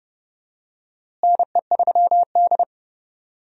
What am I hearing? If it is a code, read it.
Morse code, heard as a keyed tone, about 30 wpm, 700 Hz. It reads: NE3D
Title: USA Call Signs - 4 Characters - Repeated 2x - 30wpm